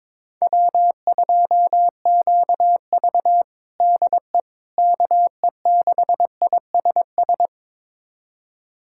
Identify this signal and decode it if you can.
Morse code, heard as a keyed tone, about 22 wpm, 700 Hz. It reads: W2QV DE KE6ISS
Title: USA Callsign DE Callsign 22wpm